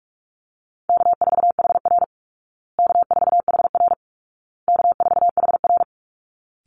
Morse code, heard as a keyed tone, about 44 wpm, 700 Hz. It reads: K4HR K4HR K4HR
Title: Easy Call Signs Repeated 3x - 45wpm